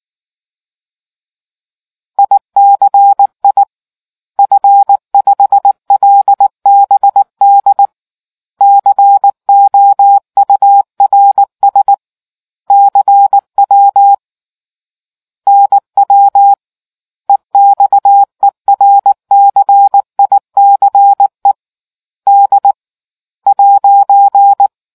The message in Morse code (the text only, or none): ICI F5LBD COURS CW NW EXERCICE D'